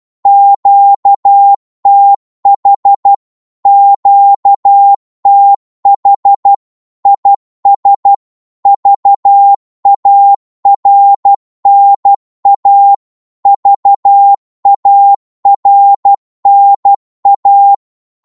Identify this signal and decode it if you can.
Morse code, heard as a keyed tone, about 12 wpm, 800 Hz. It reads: QTH QTH IS VARNA VARNA
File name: Morse / qso-12wpm